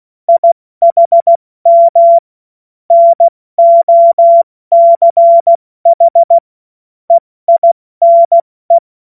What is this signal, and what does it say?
Morse code, heard as a keyed tone, about 16 wpm, 680 Hz. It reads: IHM NOCH EINE
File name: Morsekurs / Witze / Witz 28.November 2023 70 F80